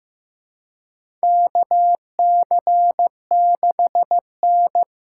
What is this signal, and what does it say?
Morse code, heard as a keyed tone, about 15 wpm, 700 Hz. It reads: KC6N
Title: USA Call Signs - 4 Characters - 3x Character Spacing - 15wpm